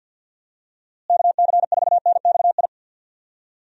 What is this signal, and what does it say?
Morse code, heard as a keyed tone, about 46 wpm, 700 Hz. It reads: KC4NXI
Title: USA Call Signs - 6 Characters - Repeated 2x - 50wpm